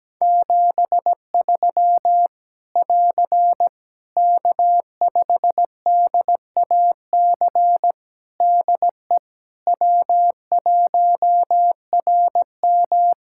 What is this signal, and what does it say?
Morse code, heard as a keyed tone, about 17 wpm, 700 Hz. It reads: 73 <AR> K5DAC DE W1RM
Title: Practice QSO - Exchange 4 - Ending - 17wpm